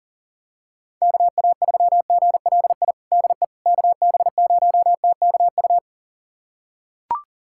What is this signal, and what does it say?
Morse code, heard as a keyed tone, about 40 wpm, 700 Hz. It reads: KA3GLI DE KB0TKU
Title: USA Callsign DE Callsign - 6 Characters - 40wpm